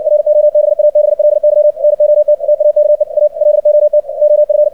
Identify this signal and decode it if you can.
Morse code, heard as a keyed tone, about 60 wpm, 600 Hz, with radio noise. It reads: ROBABLY NOT TAKE A LOT OF
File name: archived / CRSnet